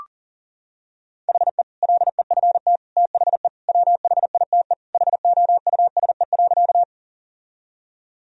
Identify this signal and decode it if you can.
Morse code, heard as a keyed tone, about 40 wpm, 700 Hz. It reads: HE LEFT THE WHITE HOUSE.
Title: Sentences from Top 1000 Words - Review Part 1 - 40wpm